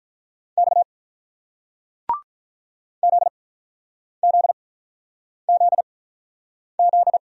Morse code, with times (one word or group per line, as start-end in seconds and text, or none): 0.6-0.8 X
3.0-7.2 ZZZZ